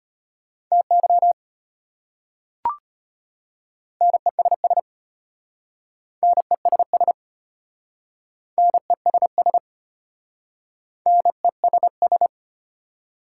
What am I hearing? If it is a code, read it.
Morse code, heard as a keyed tone, about 33 wpm, 700 Hz. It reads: TY NESS NESS NESS NESS